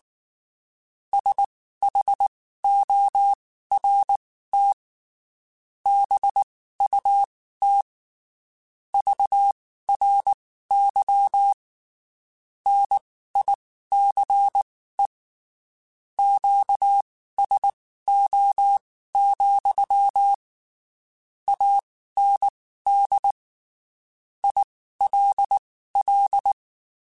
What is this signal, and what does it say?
Morse code, heard as a keyed tone, about 19 wpm, spaced as for 14 wpm, 800 Hz. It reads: SHORT BUT VRY NICE QSO, AND ILL
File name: (spaced) / QSOs Spaced-mp3s / WA7QOX 2 @18 sp